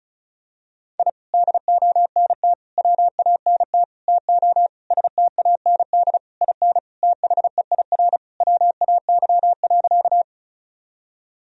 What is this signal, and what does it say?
Morse code, heard as a keyed tone, about 36 wpm, 700 Hz. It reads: I DONT WANT TO STAND IN THEIR WAY.
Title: Sentences from Top 1000 Words - Review Part 1 - 35wpm